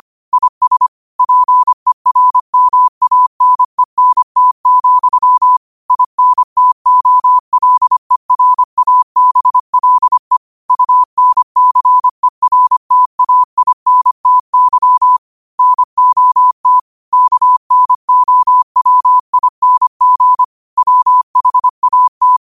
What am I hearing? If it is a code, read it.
Morse code, heard as a keyed tone, about 25 wpm, 1000 Hz. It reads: IS PERMANENT, INTOLERABLE UNCERTAINTY NOT KNOWING WHAT